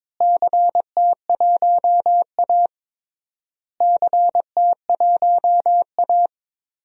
Morse code, heard as a keyed tone, about 22 wpm, 700 Hz. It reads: CT1A CT1A
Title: DXCC-100 Call Signs - 4 Characters - Repeated 3x - 22wpm